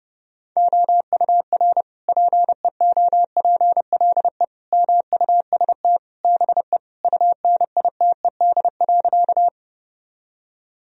Morse code, heard as a keyed tone, about 30 wpm, 700 Hz. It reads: OUR PEOPLE MUST BE UNITED.
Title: Sentences from Top 1000 Words - Review Part 4 - 30wpm